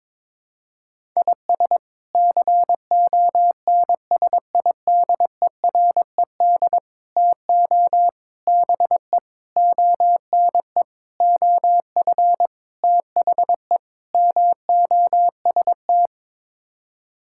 Morse code, heard as a keyed tone, about 22 wpm, 700 Hz. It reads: IS CONSIDERED TO BE ONE OF THE MOST
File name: practice / 22wpm / Sets of 8 Words - 22wpm